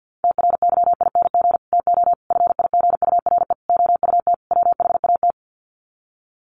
Morse code, heard as a keyed tone, about 50 wpm, 700 Hz. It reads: TRYING TO FIGURE OUT WHAT
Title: Sets of 5 Words 50wpm